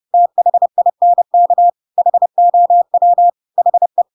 Morse code, heard as a keyed tone, about 30 wpm, 700 Hz. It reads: THINK HOW HE
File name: Pinocchio / Pinocchio ch5 30 30 ik1hge